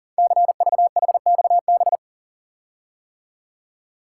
Morse code, heard as a keyed tone, about 38 wpm, 700 Hz. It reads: CVHXB
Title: Code Groups - 5 Letters - 40wpm